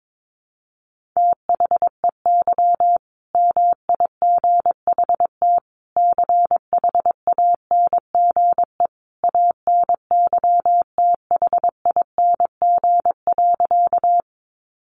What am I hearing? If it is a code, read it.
Morse code, heard as a keyed tone, about 22 wpm, 700 Hz. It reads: THEY MIGHT CHANGE ANYTHING.
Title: A Sound Of Thunder 22wpm